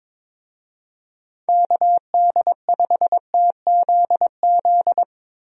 Morse code, heard as a keyed tone, about 22 wpm, 700 Hz. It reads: KD5TZZ
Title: USA Call Signs - 6 Characters - Repeated 2x - 22wpm